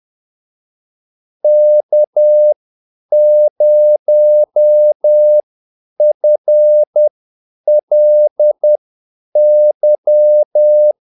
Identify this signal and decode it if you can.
Morse code, heard as a keyed tone, about 10 wpm, spaced as for 8 wpm, 600 Hz. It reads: K0FLY
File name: Callsigns-10-8wpm